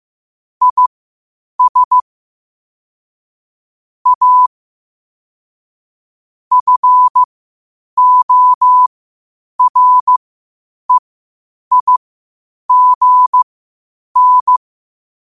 Morse code, heard as a keyed tone, about 15 wpm, spaced as for 8 wpm, 1000 Hz. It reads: IS A FOREIGN
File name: qotdfiles / 8wpm